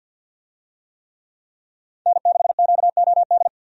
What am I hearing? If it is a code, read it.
Morse code, heard as a keyed tone, about 47 wpm, 700 Hz. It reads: N6QOD
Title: USA Call Signs - 5 Characters - 50wpm